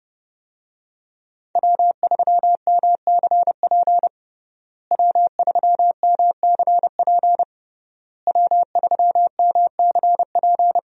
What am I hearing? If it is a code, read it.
Morse code, heard as a keyed tone, about 30 wpm, 700 Hz. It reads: W3MCP W3MCP W3MCP
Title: USA Call Signs - 5 Characters - Repeated 3x - 30wpm